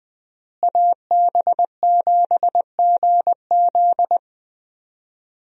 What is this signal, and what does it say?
Morse code, heard as a keyed tone, about 20 wpm, 700 Hz. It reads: AB7GZ
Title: USA Call Signs - 5 Characters - Repeated 2x - 20wpm